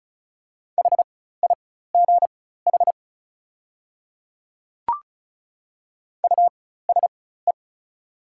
Morse code, heard as a keyed tone, about 34 wpm, spaced as for 16 wpm, 690 Hz. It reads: HIGH USE
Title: 4x Character Spacing - Top 300 Words - 35wpm